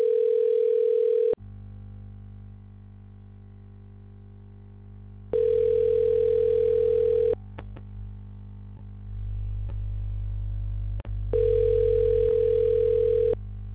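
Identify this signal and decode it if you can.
Ringback tone (North America), 440 + 480 Hz.
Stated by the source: Outbound phone call reaching voicemail, recorded by an outbound call service.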